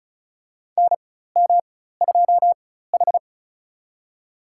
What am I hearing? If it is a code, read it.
Morse code, heard as a keyed tone, about 34 wpm, spaced as for 16 wpm, 700 Hz. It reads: NM2H
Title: USA Call Signs - 4 Characters - 4x Character Spacing - 35wpm